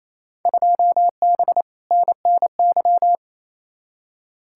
Morse code, heard as a keyed tone, about 28 wpm, 700 Hz. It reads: 2B NNY
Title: Field Day Exchanges Encore - 28wpm